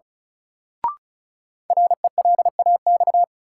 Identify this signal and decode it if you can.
Morse code, heard as a keyed tone, about 36 wpm, 700 Hz. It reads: RELAX
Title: Sentences from Top 1400 Words - New Word Drill - 35wpm